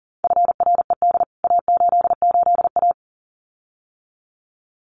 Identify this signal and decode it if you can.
Morse code, heard as a keyed tone, about 40 wpm, 700 Hz. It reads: FRED A88A
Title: CWT Exchange 40wpm